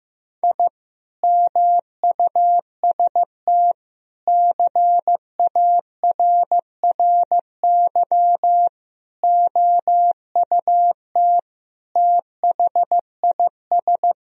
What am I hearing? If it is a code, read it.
Morse code, heard as a keyed tone, about 15 wpm, 700 Hz. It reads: I MUST CARRY OUT THIS
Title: Sentences from Top 1000 Words - Review Part 2 - 15wpm